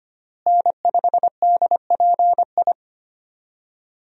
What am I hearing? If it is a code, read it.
Morse code, heard as a keyed tone, about 25 wpm, 700 Hz. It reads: N5DPI
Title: USA Call Signs - 5 Characters - Repeated 3x - 25wpm